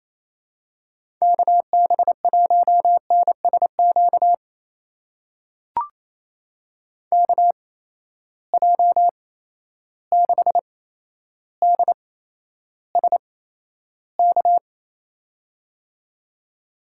Morse code, heard as a keyed tone, about 28 wpm, spaced as for 12 wpm, 700 Hz. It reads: KB1NSQ KJ6DSK
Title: USA Call Signs - 6 Characters - 8x Character Spacing - 28wpm